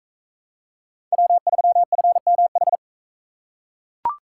Morse code, heard as a keyed tone, about 40 wpm, 700 Hz. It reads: W3FMH